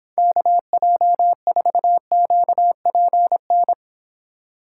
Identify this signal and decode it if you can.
Morse code, heard as a keyed tone, about 26 wpm, 700 Hz. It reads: KJ4QPN